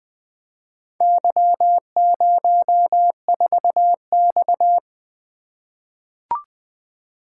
Morse code, heard as a keyed tone, about 20 wpm, 700 Hz. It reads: Y04X
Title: Four Letter-Number Combinations - 20wpm